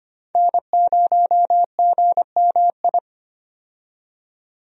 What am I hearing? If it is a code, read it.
Morse code, heard as a keyed tone, about 25 wpm, 700 Hz. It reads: N0GMI
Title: USA Call Signs - 5 Characters - 25wpm